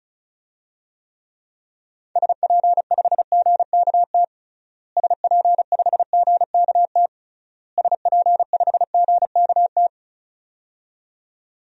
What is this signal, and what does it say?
Morse code, heard as a keyed tone, about 36 wpm, 700 Hz. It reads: SP5GKT SP5GKT SP5GKT
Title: DXCC-100 Call Signs - 6 Characters - Repeated 3x - 35wpm